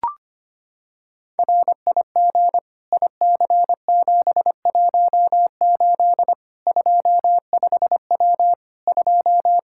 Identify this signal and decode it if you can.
Morse code, heard as a keyed tone, about 25 wpm, 700 Hz. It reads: RIG IC718 25W 2
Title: Practice QSO - Exchange 2 - Info - 25wpm